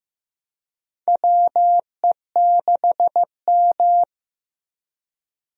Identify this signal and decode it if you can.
Morse code, heard as a keyed tone, about 15 wpm, 700 Hz. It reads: WE6M
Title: USA Call Signs - 4 Characters - Repeated 3x - 15wpm